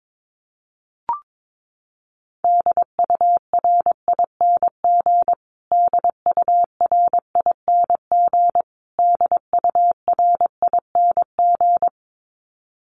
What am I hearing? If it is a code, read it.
Morse code, heard as a keyed tone, about 22 wpm, 700 Hz. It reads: DURING DURING DURING
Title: Top 500 Words Repeated 22wpm